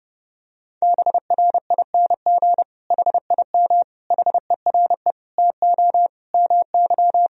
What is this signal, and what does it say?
Morse code, heard as a keyed tone, about 30 wpm, 700 Hz. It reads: BRING HIM HERE TO MY
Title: Sentences from Top 1000 Words - Review Part 3 - 30wpm